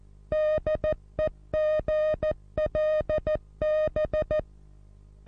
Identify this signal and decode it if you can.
Morse code, heard as a keyed tone, about 14 wpm, 620 Hz. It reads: DEGLB
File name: repos / Morsekurs / DJ4UF Morsen27